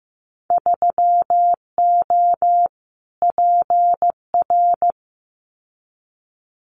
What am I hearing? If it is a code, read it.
Morse code, heard as a keyed tone, about 15 wpm, 700 Hz. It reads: 3O PR
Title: Winter Field Day Exchanges Encore 15wpm